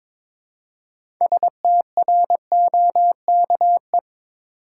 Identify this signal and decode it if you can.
Morse code, heard as a keyed tone, about 22 wpm, 700 Hz. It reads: STROKE
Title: Six Letter Words - 22wpm